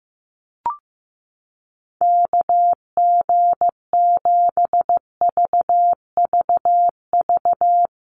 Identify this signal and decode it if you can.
Morse code, heard as a keyed tone, about 15 wpm, 700 Hz. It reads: KG7VVV
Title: USA Call Signs Repeated 3x 15wpm